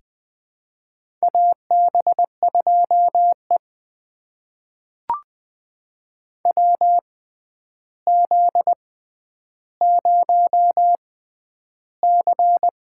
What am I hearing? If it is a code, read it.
Morse code, heard as a keyed tone, about 20 wpm, spaced as for 10 wpm, 700 Hz. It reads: AB2E WZ0C
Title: USA Call Signs - 4 Characters - 6x Character Spacing - 20wpm